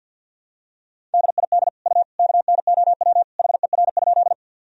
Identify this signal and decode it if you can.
Morse code, heard as a keyed tone, about 50 wpm, 700 Hz. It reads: DID U KNOW HER?
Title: Sentences from Top 100 Words and Abbreviations 50wpm